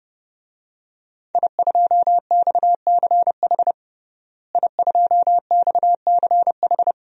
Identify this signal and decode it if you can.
Morse code, heard as a keyed tone, about 30 wpm, 700 Hz. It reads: I2XCH I2XCH
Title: DXCC-100 Call Signs - 5 Characters - Repeated 3x - 30wpm